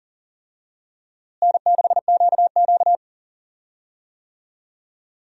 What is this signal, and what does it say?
Morse code, heard as a keyed tone, about 40 wpm, 700 Hz. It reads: N6QQ
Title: USA Call Signs - 4 Characters - Repeated 2x - 40wpm